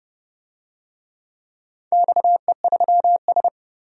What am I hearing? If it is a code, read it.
Morse code, heard as a keyed tone, about 30 wpm, 700 Hz. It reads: XE3S
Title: DXCC-100 Call Signs - 4 Characters - 30wpm